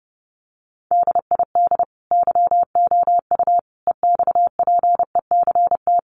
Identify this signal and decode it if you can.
Morse code, heard as a keyed tone, about 30 wpm, 700 Hz. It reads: DID YOU EXPECT
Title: Sentences from Top 1100 Words 30wpm